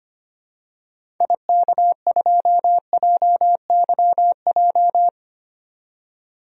Morse code, heard as a keyed tone, about 25 wpm, 700 Hz. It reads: IK2JYJ
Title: DXCC-100 Call Signs - 6 Characters - 25wpm